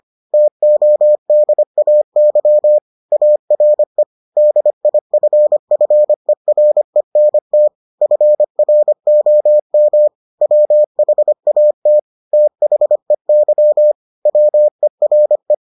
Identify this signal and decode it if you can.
Morse code, heard as a keyed tone, about 25 wpm, 600 Hz. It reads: TODAY ARE DIFFERENT FROM WHAT THEY WERE